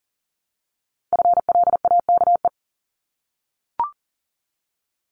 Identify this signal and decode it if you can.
Morse code, heard as a keyed tone, about 41 wpm, 700 Hz. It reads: FLAKE
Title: Multi-letter Phonemes Repeated 3x 40wpm